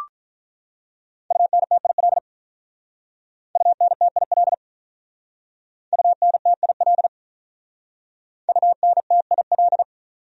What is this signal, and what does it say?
Morse code, heard as a keyed tone, about 45 wpm, 700 Hz. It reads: UNTIL UNTIL UNTIL UNTIL